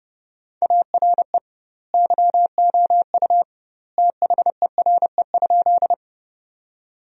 Morse code, heard as a keyed tone, about 30 wpm, 700 Hz. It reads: ARE YOU THERE?
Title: 2x Word-Spacing - Sentences from Top 100 Words - 30wpm